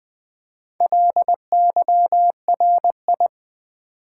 Morse code, heard as a keyed tone, about 20 wpm, 700 Hz. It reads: LYRI
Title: Build A Word - 1 Letter At A Time - 5-Letter Words - 20wpm